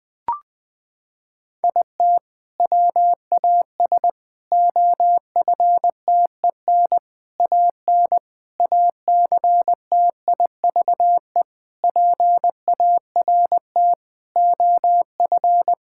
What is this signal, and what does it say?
Morse code, heard as a keyed tone, about 20 wpm, 700 Hz. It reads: IT WAS OFTEN AN ACTIVE PART OF
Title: Sentences from Top 1000 Words - Review Part 6 - 20wpm